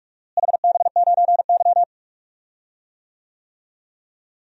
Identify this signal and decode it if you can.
Morse code, heard as a keyed tone, about 46 wpm, 700 Hz. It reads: HB9Y
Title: DXCC-100 Call Signs - 4 Characters - Repeated 3x - 45wpm